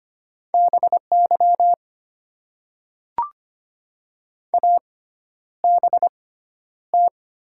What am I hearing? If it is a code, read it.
Morse code, heard as a keyed tone, about 25 wpm, spaced as for 11 wpm, 700 Hz. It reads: BY ABT
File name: Top 90 QSO Elements - 6x Character Spacing - 25wpm